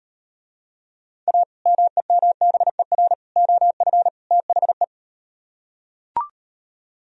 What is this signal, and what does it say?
Morse code, heard as a keyed tone, about 38 wpm, 700 Hz. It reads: A MEMBER OF THE